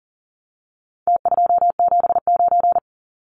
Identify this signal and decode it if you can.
Morse code, heard as a keyed tone, about 40 wpm, 700 Hz. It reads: T279
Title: Numbers - Contest Style - 4-Digit 40wpm